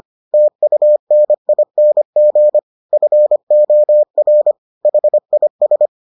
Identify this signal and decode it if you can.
Morse code, heard as a keyed tone, about 25 wpm, 600 Hz. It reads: TUNING FOR HIS